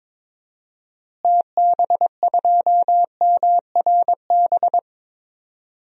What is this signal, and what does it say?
Morse code, heard as a keyed tone, about 22 wpm, 700 Hz. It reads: TB2MRB
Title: DXCC-100 Call Signs - 6 Characters - Repeated 2x - 22wpm